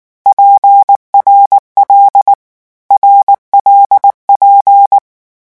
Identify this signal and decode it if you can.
Morse code, heard as a keyed tone, about 19 wpm, 800 Hz. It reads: PRL RLP
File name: morse mp3 / lecon6-vitesse 18